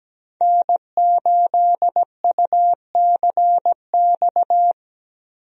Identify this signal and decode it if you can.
Morse code, heard as a keyed tone, about 17 wpm, 700 Hz. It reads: N8UCX
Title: USA Call Signs - 5 Characters - 17wpm